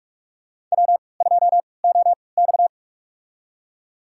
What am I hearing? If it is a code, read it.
Morse code, heard as a keyed tone, about 44 wpm, spaced as for 25 wpm, 700 Hz. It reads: W2OX
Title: USA Call Signs - 4 Characters - 3x Character Spacing - 45wpm